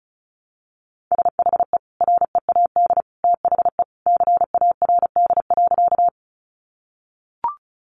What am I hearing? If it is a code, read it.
Morse code, heard as a keyed tone, about 34 wpm, 700 Hz. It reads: SHE READ THE CARD.
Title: Sentences from Top 600 Words 35wpm